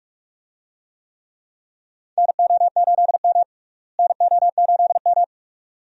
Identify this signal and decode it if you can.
Morse code, heard as a keyed tone, about 41 wpm, 700 Hz. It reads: NO8M NO8M
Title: USA Call Signs - 4 Characters - Repeated 2x - 45wpm